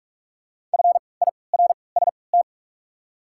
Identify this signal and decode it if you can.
Morse code, heard as a keyed tone, about 44 wpm, spaced as for 25 wpm, 690 Hz. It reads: FIRST
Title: 3x Character Spacing - Top 100 Words - 45wpm